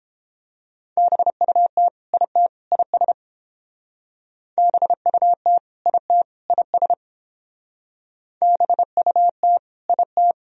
Morse code, heard as a keyed tone, about 30 wpm, 700 Hz. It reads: BUT IT IS BUT IT IS BUT IT